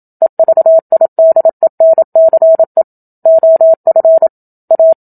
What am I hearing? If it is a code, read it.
Morse code, heard as a keyed tone, about 27 wpm, 650 Hz. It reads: EVIDENCE OF A
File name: Book Two / 01-Chapter 1 - 27 WPM